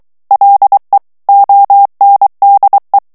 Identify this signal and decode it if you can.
Morse code, heard as a keyed tone, about 23 wpm, 800 Hz. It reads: LE ONDE
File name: morse mp3 / ITALIEN texte clair 1 vitesse 22 mots minute